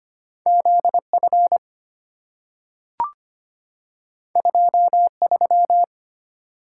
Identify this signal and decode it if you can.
Morse code, heard as a keyed tone, about 25 wpm, 690 Hz. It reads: ZF 23
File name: Two Letter-Number Combinations - 25wpm